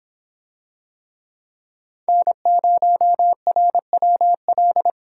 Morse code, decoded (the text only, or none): N0RWL